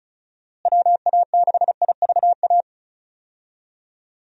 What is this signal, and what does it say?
Morse code, heard as a keyed tone, about 34 wpm, 700 Hz. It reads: WA6IVA
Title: USA Call Signs - 6 Characters - Repeated 3x - 35wpm